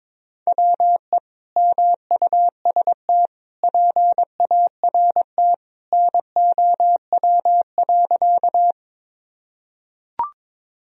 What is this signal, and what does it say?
Morse code, heard as a keyed tone, about 22 wpm, 700 Hz. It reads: WE MUST PART NOW.
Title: Sentences from Top 1000 Words - Review Part 1 - 22wpm